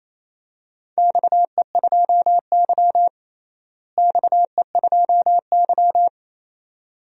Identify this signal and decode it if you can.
Morse code, heard as a keyed tone, about 28 wpm, 700 Hz. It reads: XE2Y XE2Y
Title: DXCC-100 Call Signs - 4 Characters - Repeated 2x - 28wpm